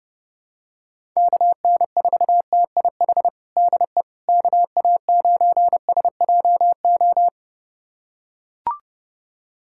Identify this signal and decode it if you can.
Morse code, heard as a keyed tone, about 30 wpm, 700 Hz. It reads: KN4TIH DE KA9SJO
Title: USA Callsign DE Callsign - 6 Characters - 30wpm